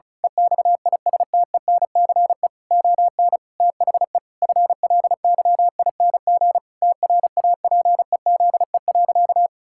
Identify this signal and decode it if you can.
Morse code, heard as a keyed tone, about 36 wpm, 700 Hz. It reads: EXISTENCE ON THE FLYING TRAPEZE.
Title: Pangram Challenge v2 - 35wpm